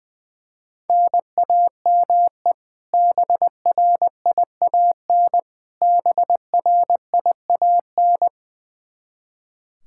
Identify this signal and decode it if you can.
Morse code, heard as a keyed tone, about 21 wpm, 700 Hz. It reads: NAME BRIAN BRIAN
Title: K1USN SST QSO Practice 20wpm